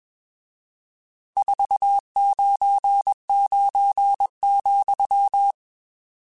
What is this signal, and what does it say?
Morse code, heard as a keyed tone, about 21 wpm, 800 Hz. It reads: 499,